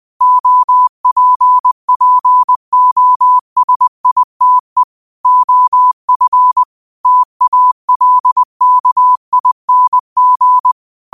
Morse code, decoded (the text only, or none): OPPOSITE OF TALKING